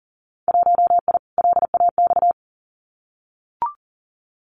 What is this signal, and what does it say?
Morse code, heard as a keyed tone, about 40 wpm, 700 Hz. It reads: 1I LAX
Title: Winter Field Day Exchanges Encore 40wpm